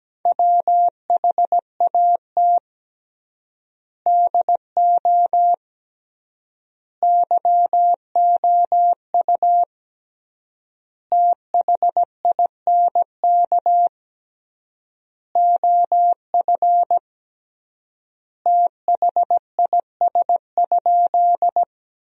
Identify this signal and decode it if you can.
Morse code, heard as a keyed tone, about 17 wpm, 700 Hz. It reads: WHAT DO YOU THINK OF THIS?
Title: 3x Word-Spacing - Sentences from Top 200 Words - 17wpm